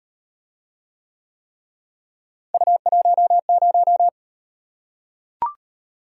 Morse code, heard as a keyed tone, about 37 wpm, 700 Hz. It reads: U10